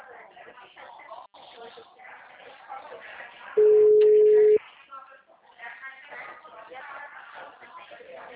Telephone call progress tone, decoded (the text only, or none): ringback tone